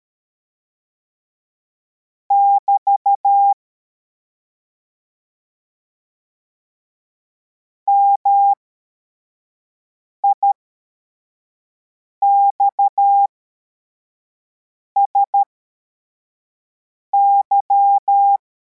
Morse code, heard as a keyed tone, about 13 wpm, spaced as for 4 wpm, 800 Hz. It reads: <BT> MIXSY